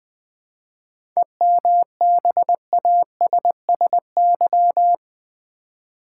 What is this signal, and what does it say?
Morse code, heard as a keyed tone, about 20 wpm, 700 Hz. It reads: EMBASSY
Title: 7 Letter Words - First 3 Letters Given - 20wpm